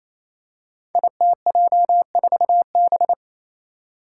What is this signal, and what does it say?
Morse code, heard as a keyed tone, about 29 wpm, 700 Hz. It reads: ITJ4B
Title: Five Challenging Letter-Number Combinations 28wpm